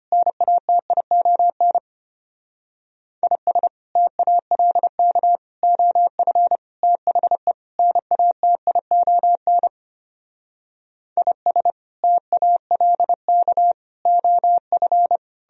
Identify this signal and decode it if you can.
Morse code, heard as a keyed tone, about 32 wpm, 700 Hz. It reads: NATION IS TALK OF THE NATION IS TALK OF